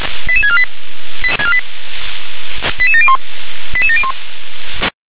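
Five-tone selective call (ZVEI), heard as quick runs of five tones, tones of about 70 ms. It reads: 88648 88701 88701
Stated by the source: Five-tone ZVEI selective call 88640 received on an SDR.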